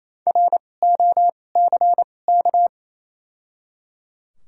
Morse code, heard as a keyed tone, about 28 wpm, spaced as for 20 wpm, 700 Hz. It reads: ROCK
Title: 2x Character Spacing - Top 500 Words - 28wpm